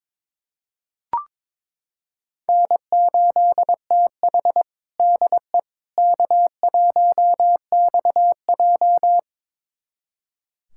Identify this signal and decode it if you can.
Morse code, heard as a keyed tone, about 22 wpm, 700 Hz. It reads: N8TH DE K1XJ
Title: USA Callsign DE Callsign - 4 Characters - 22wpm